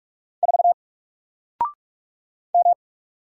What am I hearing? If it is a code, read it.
Morse code, heard as a keyed tone, about 44 wpm, 700 Hz. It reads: 4 M
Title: Single Character (Letter:Number:Punctuation:Prosign) 45wpm